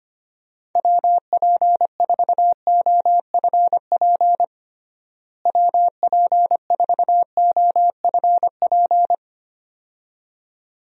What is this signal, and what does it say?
Morse code, heard as a keyed tone, about 25 wpm, 700 Hz. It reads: WP4OFP WP4OFP
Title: USA Call Signs - 6 Characters - Repeated 2x - 25wpm